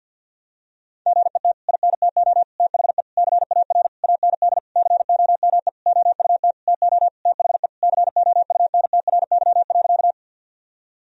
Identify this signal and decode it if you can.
Morse code, heard as a keyed tone, about 52 wpm, 700 Hz. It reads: GET INTO THE CAR AND COME OUT TO THE COUNTRY.
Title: Sentences from Top 1000 Words - Review Part 1 - 50wpm